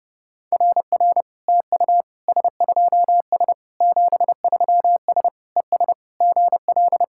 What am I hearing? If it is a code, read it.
Morse code, heard as a keyed tone, about 30 wpm, 700 Hz. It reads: RR TU S2S 73S ES GL
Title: SOTA QSO Practice 30wpm